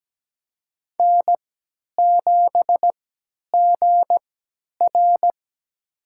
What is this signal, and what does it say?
Morse code, heard as a keyed tone, about 17 wpm, spaced as for 10 wpm, 700 Hz. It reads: N7GR
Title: USA Call Signs - 4 Characters - 3x Character Spacing - 17wpm